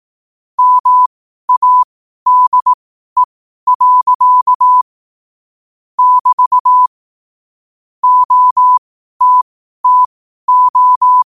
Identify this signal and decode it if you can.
Morse code, heard as a keyed tone, about 18 wpm, spaced as for 12 wpm, 1000 Hz. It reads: MADE. <BT> OTTO
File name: qotdfiles / 12wpm